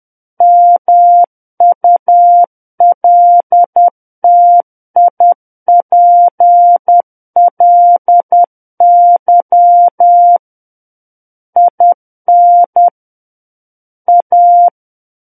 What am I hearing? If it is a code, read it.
Morse code, heard as a keyed tone, about 10 wpm, 700 Hz. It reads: MULTIPLY IN A
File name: Book One / 01-Chapter 1 - 10 WPM